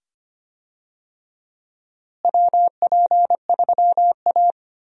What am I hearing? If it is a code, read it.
Morse code, heard as a keyed tone, about 25 wpm, 700 Hz. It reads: WP3A
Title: USA Call Signs - 4 Characters - Repeated 2x - 25wpm